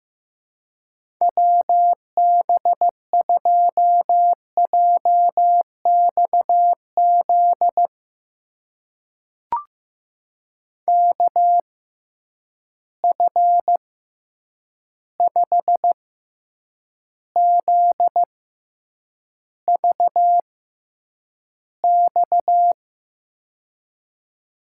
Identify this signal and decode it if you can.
Morse code, heard as a keyed tone, about 15 wpm, spaced as for 8 wpm, 700 Hz. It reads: WB2JXZ KF5ZVX
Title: USA Call Signs - 6x Character Spacing - 15wpm